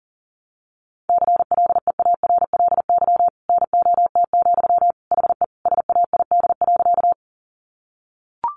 Morse code, heard as a keyed tone, about 40 wpm, 700 Hz. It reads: CLEARLY NOT, HE SAID.
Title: Sentences from Top 1300 Words 40wpm